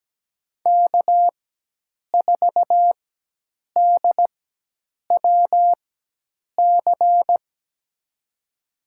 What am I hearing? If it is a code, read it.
Morse code, heard as a keyed tone, about 17 wpm, spaced as for 8 wpm, 700 Hz. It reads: K4DWC
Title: USA Call Signs - 4x Character Spacing - 17wpm